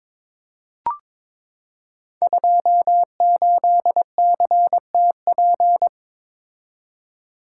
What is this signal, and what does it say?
Morse code, heard as a keyed tone, about 22 wpm, 700 Hz. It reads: 28CTP
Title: Code Groups - 5 Letter:Numbers - 22wpm